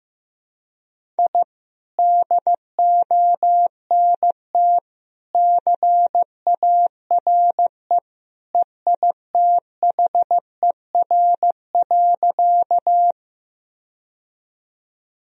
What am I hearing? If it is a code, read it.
Morse code, heard as a keyed tone, about 15 wpm, 700 Hz. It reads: I DONT CARE EITHER.
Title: Sentences from Top 1000 Words - Review Part 4 - 15wpm